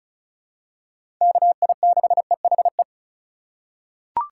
Morse code, heard as a keyed tone, about 34 wpm, 700 Hz. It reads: KI6EHE
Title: USA Call Signs - 6 Characters - 2x Character Spacing - 35wpm